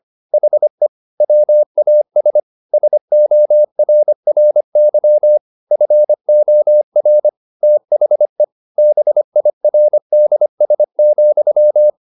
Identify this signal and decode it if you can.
Morse code, heard as a keyed tone, about 25 wpm, 600 Hz. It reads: HE WAS SORRY FOR THE BIRDS,
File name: the old man and the sea part 1 25wpm